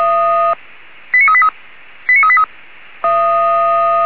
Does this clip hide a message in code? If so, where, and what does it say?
Yes: five-tone selective call (ZVEI) from 1.1 to 2.5 s, heard as quick runs of five tones, tones of about 70 ms. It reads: 89383 89383